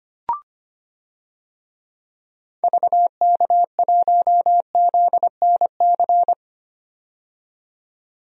Morse code, heard as a keyed tone, about 25 wpm, 700 Hz. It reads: VK1ZNC
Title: DXCC-100 Call Signs - 6 Characters - 25wpm